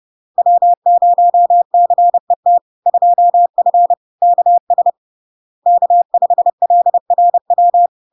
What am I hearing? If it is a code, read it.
Morse code, heard as a keyed tone, about 30 wpm, 700 Hz. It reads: W0CET 2F KS K5LRW
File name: LARK-FD-2022-Log-in-CW-30wpm-700-Hz